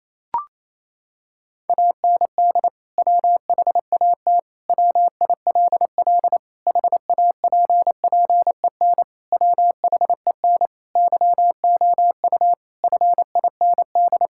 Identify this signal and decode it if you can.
Morse code, heard as a keyed tone, about 27 wpm, 700 Hz. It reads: AND WHAT WILL HAPPEN WHEN YOU FIND
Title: Sentences from Top 1000 Words - Review Part 5 - 28wpm